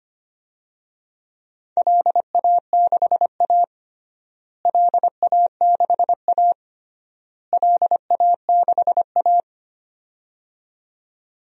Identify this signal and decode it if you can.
Morse code, heard as a keyed tone, about 25 wpm, 700 Hz. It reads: LA6A LA6A LA6A
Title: DXCC-100 Call Signs - 4 Characters - Repeated 3x - 25wpm